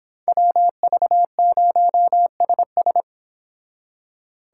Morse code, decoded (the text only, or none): WV0SS